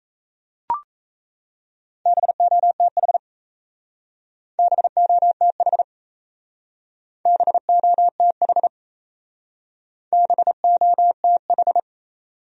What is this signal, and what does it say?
Morse code, heard as a keyed tone, about 34 wpm, 700 Hz. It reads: BOTH BOTH BOTH BOTH
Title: Speed-Racing Top 500 Words 28wpm